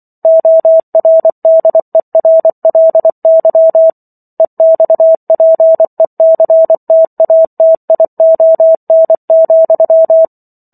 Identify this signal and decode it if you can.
Morse code, heard as a keyed tone, about 24 wpm, 650 Hz. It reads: ORDERLY EXPECTATION,